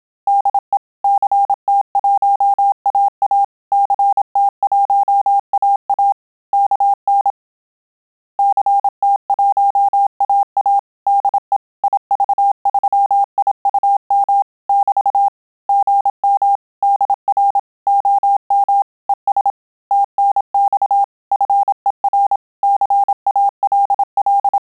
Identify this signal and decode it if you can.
Morse code, heard as a keyed tone, about 27 wpm, 800 Hz. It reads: DE CT1AA CT1AA KN CT1AA DE IV3IUM <BT> GM DR OM ES TNX FER CALL
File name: Telegraphy-group / QSO CW 25 PPM